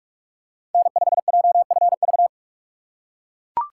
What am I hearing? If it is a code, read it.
Morse code, heard as a keyed tone, about 43 wpm, 700 Hz. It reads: N5JFV
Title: USA Call Signs - 5 Characters - 4x Character Spacing - 45wpm